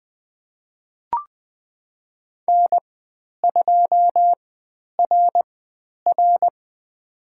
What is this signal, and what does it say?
Morse code, heard as a keyed tone, about 20 wpm, spaced as for 10 wpm, 700 Hz. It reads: N2RR
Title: 095 Lesson 16g - Letter c - Call Signs 20wpm